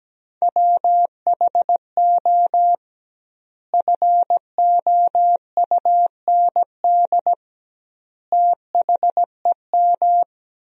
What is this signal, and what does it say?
Morse code, heard as a keyed tone, about 17 wpm, 700 Hz. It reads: WHO FOUND THEM
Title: 2x Word-Spacing - Sentences from Top 300 Words - 17wpm